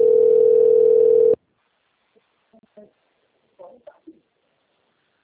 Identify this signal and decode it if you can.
Ringback tone (North America), 440 + 480 Hz, with speech.
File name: rg-1600-unknown- - - .36462